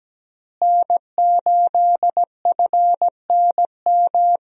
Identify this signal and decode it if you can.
Morse code, heard as a keyed tone, about 17 wpm, 700 Hz. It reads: N8FNM
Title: USA Call Signs - 5 Characters - Repeated 3x - 17wpm